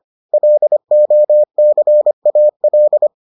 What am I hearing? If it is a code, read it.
Morse code, heard as a keyed tone, about 25 wpm, 600 Hz. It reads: LOCAL